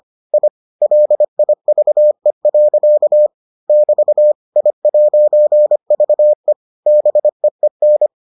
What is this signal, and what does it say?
Morse code, heard as a keyed tone, about 25 wpm, 600 Hz. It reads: I LIVE. <BT> I'VE BEEN